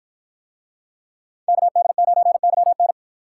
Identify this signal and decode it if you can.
Morse code, heard as a keyed tone, about 53 wpm, 700 Hz. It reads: KD9YN